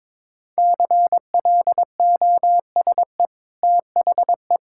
Morse code, heard as a keyed tone, about 22 wpm, 690 Hz. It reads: CLOSE THE
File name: practice / 22wpm / Easy Sentences with Hints - 22wpm